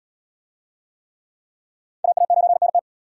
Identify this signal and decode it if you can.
Morse code, heard as a keyed tone, about 74 wpm, 700 Hz. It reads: AA9NT